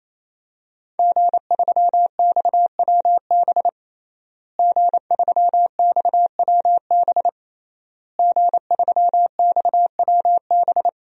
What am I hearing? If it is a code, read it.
Morse code, heard as a keyed tone, about 27 wpm, 700 Hz. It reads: G3XWB G3XWB G3XWB
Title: DXCC-100 Call Signs - 5 Characters - Repeated 3x - 28wpm